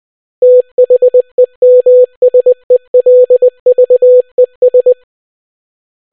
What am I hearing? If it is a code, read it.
Morse code, heard as a keyed tone, about 20 wpm, 500 Hz. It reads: THEMSELVES